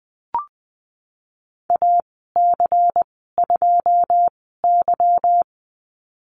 Morse code, heard as a keyed tone, about 20 wpm, spaced as for 15 wpm, 700 Hz. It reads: AC2Y
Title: 186 Lesson 30g - Letter b - Call Signs 20wpm